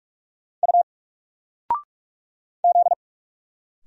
Morse code, heard as a keyed tone, about 46 wpm, 700 Hz. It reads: U Z
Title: Single Letter-Number Alt 45wpm